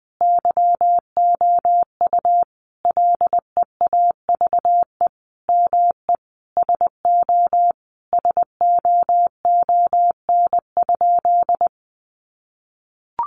Morse code, heard as a keyed tone, about 20 wpm, 700 Hz. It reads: YOU LEAVE ME SO SOON?
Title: Sentences from Top 400 Words 20wpm